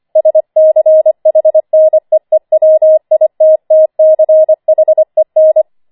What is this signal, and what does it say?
Morse code, heard as a keyed tone, about 25 wpm, 630 Hz. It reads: SCHNEEWITTCHEN
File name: 25wpm